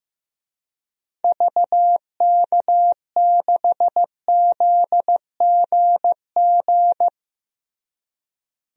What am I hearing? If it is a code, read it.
Morse code, heard as a keyed tone, about 15 wpm, 700 Hz. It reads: VK6ZGG
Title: DXCC-100 Call Signs - 6 Characters - Repeated 2x - 15wpm